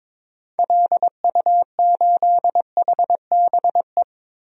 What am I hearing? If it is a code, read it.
Morse code, heard as a keyed tone, about 22 wpm, 700 Hz. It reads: LU8HBE
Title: DXCC-100 Call Signs - 6 Characters - Repeated 3x - 22wpm